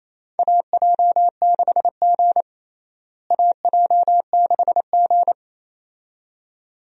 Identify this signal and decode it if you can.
Morse code, heard as a keyed tone, about 28 wpm, 700 Hz. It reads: AJ6G AJ6G
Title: USA Call Signs - 4 Characters - Repeated 2x - 28wpm